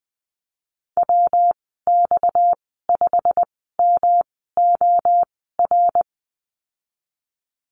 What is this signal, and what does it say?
Morse code, heard as a keyed tone, about 20 wpm, spaced as for 15 wpm, 700 Hz. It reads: WX5MOR
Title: 231 Lesson 37g - Letter x - Call Signs 20wpm